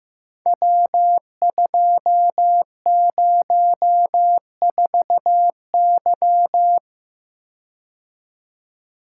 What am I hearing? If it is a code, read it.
Morse code, heard as a keyed tone, about 15 wpm, 700 Hz. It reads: W204Y
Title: Code Groups - 5 Letter:Numbers - 15wpm